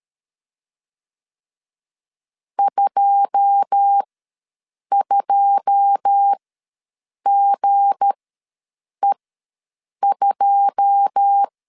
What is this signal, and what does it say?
Morse code, heard as a keyed tone, about 13 wpm, spaced as for 7 wpm, 800 Hz. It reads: 22GE2
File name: LEZIONI / 9 2 ws 12wpm 3406